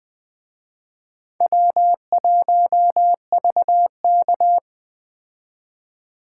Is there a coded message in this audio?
Yes: Morse code from 1.4 to 4.6 s, heard as a keyed tone, about 20 wpm, 700 Hz. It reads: W1VK